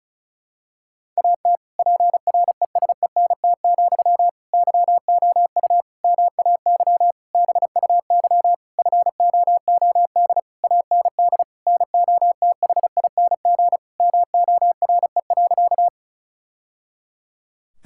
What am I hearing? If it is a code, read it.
Morse code, heard as a keyed tone, about 36 wpm, 700 Hz. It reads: AT PRESENT, YOU MAY BUY FOOD AND NOTHING MORE.
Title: Sentences from Top 1000 Words - Review Part 5 - 35wpm